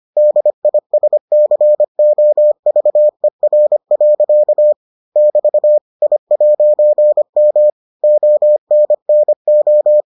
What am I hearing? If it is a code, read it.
Morse code, heard as a keyed tone, about 25 wpm, 600 Hz. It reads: DISCOVER. <BT> I'M ONNO